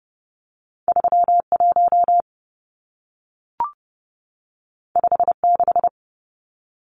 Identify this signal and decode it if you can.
Morse code, heard as a keyed tone, about 30 wpm, 700 Hz. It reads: 31 56
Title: Numbers - 2-Digit 30wpm